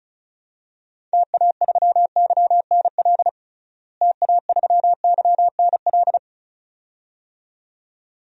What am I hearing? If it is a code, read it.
Morse code, heard as a keyed tone, about 34 wpm, 700 Hz. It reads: TA3YNL TA3YNL
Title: DXCC-100 Call Signs - 6 Characters - Repeated 2x - 35wpm